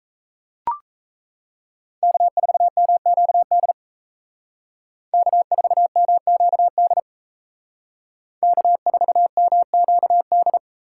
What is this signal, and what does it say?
Morse code, heard as a keyed tone, about 33 wpm, 700 Hz. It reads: K4MQD K4MQD K4MQD
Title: Speed Racing - USA 1x3 Call Signs 28wpm